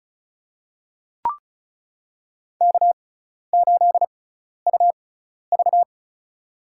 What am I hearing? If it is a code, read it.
Morse code, heard as a keyed tone, about 34 wpm, spaced as for 12 wpm, 700 Hz. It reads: K8UV
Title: USA Call Signs - 4 Characters - 6x Character Spacing - 35wpm